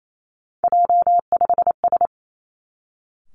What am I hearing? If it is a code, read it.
Morse code, heard as a keyed tone, about 28 wpm, 700 Hz. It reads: J5S
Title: Three Challenging Letter-Number Combinations 28wpm